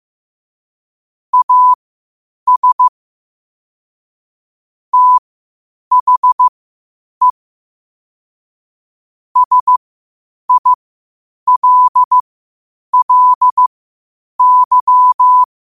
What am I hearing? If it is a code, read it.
Morse code, heard as a keyed tone, about 15 wpm, spaced as for 8 wpm, 1000 Hz. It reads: AS THE SILLY
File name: qotdfiles / 8wpm